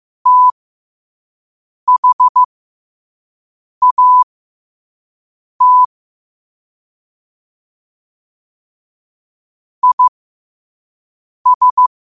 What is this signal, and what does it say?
Morse code, heard as a keyed tone, about 15 wpm, spaced as for 5 wpm, 1000 Hz. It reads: THAT IS
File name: qotdfiles / 5wpm